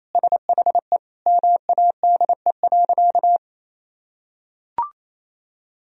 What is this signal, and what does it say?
Morse code, heard as a keyed tone, about 28 wpm, 700 Hz. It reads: SHE MADE.
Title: Sentences from Top 1000 Words - Review Part 2 - 28wpm